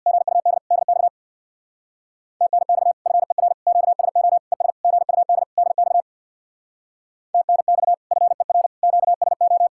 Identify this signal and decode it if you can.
Morse code, heard as a keyed tone, about 64 wpm, 700 Hz. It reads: GUD DX TNX FER QSO ES GUD DX TNX FER QSO